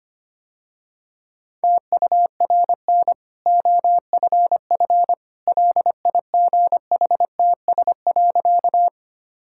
Morse code, heard as a keyed tone, about 25 wpm, 700 Hz. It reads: TURN OFF LIGHTS.
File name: Easy Sentences with Hints - 25wpm